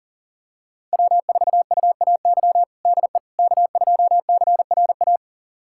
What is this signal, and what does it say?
Morse code, heard as a keyed tone, about 40 wpm, 700 Hz. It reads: W4UAY DE K2CRA
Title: USA Callsign DE Callsign - 5 Characters - 40wpm